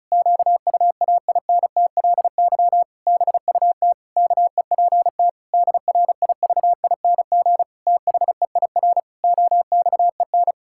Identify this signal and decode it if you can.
Morse code, heard as a keyed tone, about 34 wpm, 700 Hz. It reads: QUAINTLY BUT KEPT DRIVING THEIR OXEN
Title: Pangram Challenge v2 - 35wpm